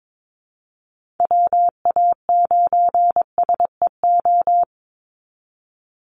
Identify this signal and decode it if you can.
Morse code, heard as a keyed tone, about 22 wpm, 700 Hz. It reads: WA9SEO
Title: Call Signs Repeated 2x - 22wpm